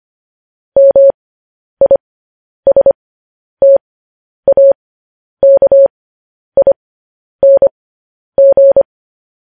Morse code, heard as a keyed tone, about 25 wpm, spaced as for 10 wpm, 550 Hz. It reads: MISTAKING
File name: podcast / News headlines-1x-25x10